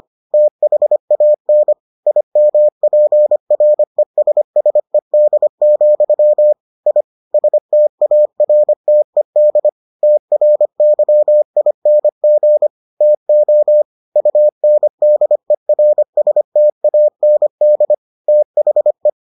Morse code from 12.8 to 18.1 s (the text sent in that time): TO UNDERSTAND